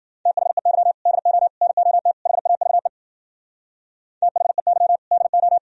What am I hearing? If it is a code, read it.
Morse code, heard as a keyed tone, about 60 wpm, 700 Hz. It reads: THEY DO NOT HAVE THEY DO